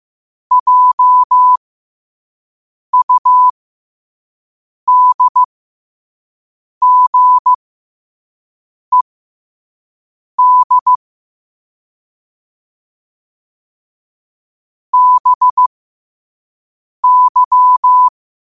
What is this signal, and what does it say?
Morse code, heard as a keyed tone, about 15 wpm, spaced as for 5 wpm, 1000 Hz. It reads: JUDGED BY
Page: New quotes every day in morse code at 5 Words per minute.